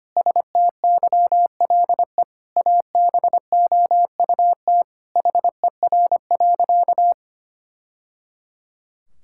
Morse code, heard as a keyed tone, about 25 wpm, 700 Hz. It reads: STYLE ABOUT HER.
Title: Sentences from Top 1000 Words - Review Part 5 - 25wpm